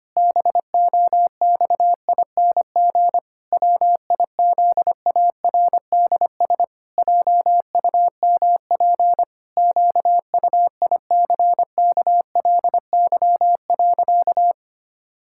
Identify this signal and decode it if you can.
Morse code, heard as a keyed tone, about 25 wpm, 700 Hz. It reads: BOXING WIZARDS JUMP QUICKLY.
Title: Pangram Challenge v2 - 25wpm